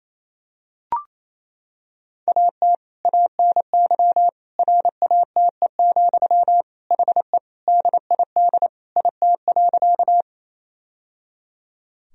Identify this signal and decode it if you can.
Morse code, heard as a keyed tone, about 28 wpm, 700 Hz. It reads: AT ANY RATE, HE DID IT.
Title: Sentences from Top 1000 Words - Review Part 4 - 28wpm